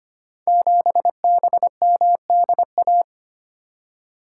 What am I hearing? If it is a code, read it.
Morse code, heard as a keyed tone, about 25 wpm, 700 Hz. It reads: 7BMDA
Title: Code Groups - 5 Letter:Numbers - 25wpm